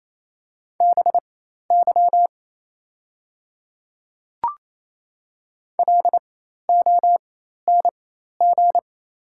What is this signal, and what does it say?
Morse code, heard as a keyed tone, about 28 wpm, spaced as for 13 wpm, 700 Hz. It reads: BY LONG
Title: 4x Character Spacing - Top 200 Words - 28wpm